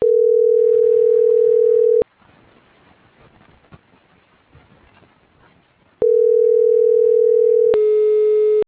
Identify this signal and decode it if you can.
Ringback tone (North America), 440 + 480 Hz.